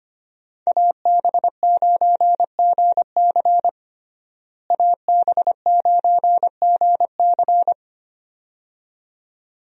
Morse code, heard as a keyed tone, about 25 wpm, 700 Hz. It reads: AB9GC AB9GC
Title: USA Call Signs - 5 Characters - Repeated 2x - 25wpm